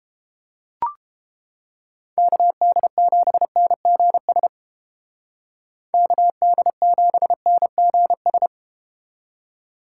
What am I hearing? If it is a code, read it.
Morse code, heard as a keyed tone, about 30 wpm, 700 Hz. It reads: KD7NGS KD7NGS